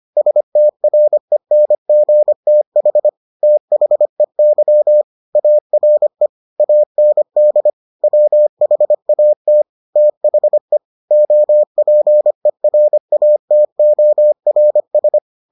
Morse code, heard as a keyed tone, about 25 wpm, 600 Hz. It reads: STRENGTH THEY ARE AND WHAT THE OPERATORS